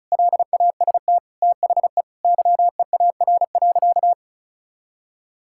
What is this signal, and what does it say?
Morse code, heard as a keyed tone, about 34 wpm, 700 Hz. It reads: LAST THE YEAR.
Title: Sentences from Top 1600 Words 35wpm